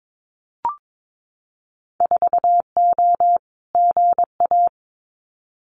Morse code, heard as a keyed tone, about 22 wpm, 700 Hz. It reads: 4O GA
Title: Winter Field Day Exchanges 22wpm